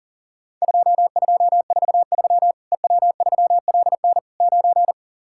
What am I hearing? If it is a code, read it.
Morse code, heard as a keyed tone, about 37 wpm, 700 Hz. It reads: 2243 EW3LN 9
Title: Worked All Europe DX Contest - QTC Exchange - 40wpm